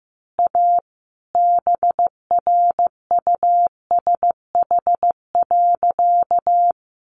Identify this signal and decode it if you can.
Morse code, heard as a keyed tone, about 15 wpm, 700 Hz. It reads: A BRUSH.
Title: Sentences from Top 1100 Words 15wpm